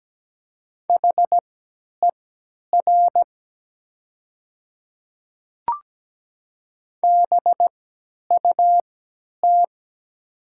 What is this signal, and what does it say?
Morse code, heard as a keyed tone, about 17 wpm, spaced as for 10 wpm, 700 Hz. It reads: HER BUT
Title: 3x Character Spacing - Top 100 Words - 17wpm